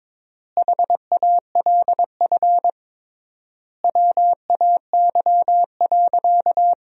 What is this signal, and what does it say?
Morse code, heard as a keyed tone, about 22 wpm, 700 Hz. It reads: HALF WAY.
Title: 3x Word-Spacing - Sentences from Top 400 Words - 22wpm